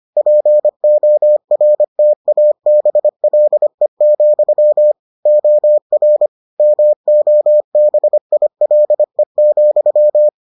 Morse code, heard as a keyed tone, about 25 wpm, 600 Hz. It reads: PORTABLE, OR MOBILE,